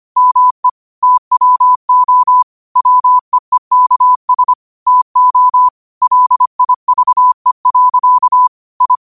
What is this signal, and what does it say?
Morse code, heard as a keyed tone, about 25 wpm, 1000 Hz. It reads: ME TWO WEEKS TO LIVE. I